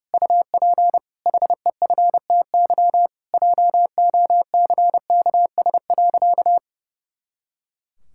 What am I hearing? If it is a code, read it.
Morse code, heard as a keyed tone, about 28 wpm, 700 Hz. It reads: UP HEFTY JOCKS.
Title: Pangram Challenge v2 - 30wpm